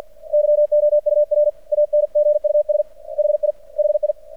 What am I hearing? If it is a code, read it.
Morse code, heard as a keyed tone, about 50 wpm, 600 Hz. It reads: YOUR ANGUS HI HI
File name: archived / CRSnet